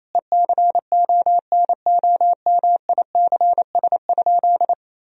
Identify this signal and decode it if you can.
Morse code, heard as a keyed tone, about 28 wpm, 700 Hz. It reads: ECONOMICS?
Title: Sentences from Top 1000 Words - Review Part 3 - 28wpm